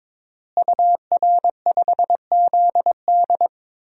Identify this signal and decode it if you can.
Morse code, heard as a keyed tone, about 22 wpm, 700 Hz. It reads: UR5ZD
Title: DXCC-100 Call Signs - 5 Characters - Repeated 2x - 22wpm